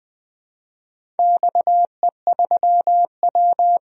Morse code, heard as a keyed tone, about 20 wpm, 700 Hz. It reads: XE3W
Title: DXCC-100 Call Signs - 4 Characters - 20wpm